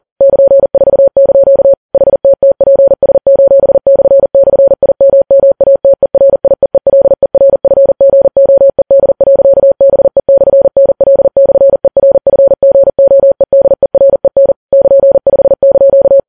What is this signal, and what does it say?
Morse code, heard as a keyed tone, about 40 wpm, 550 Hz. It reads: <KN>4) HTTPS://IMMATERIEELERFGOED.BE/NL/ERFGOEDEREN <KN>5)